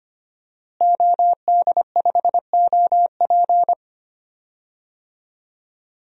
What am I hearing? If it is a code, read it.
Morse code, heard as a keyed tone, about 25 wpm, 700 Hz. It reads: OD5OP
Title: DXCC-100 Call Signs - 5 Characters - Repeated 2x - 25wpm